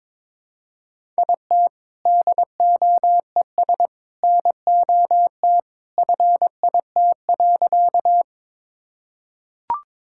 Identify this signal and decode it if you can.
Morse code, heard as a keyed tone, about 22 wpm, 700 Hz. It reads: IT DOES NOT FIT.
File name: practice / 22wpm / Sentences with 2-Word Phrases - 22wpm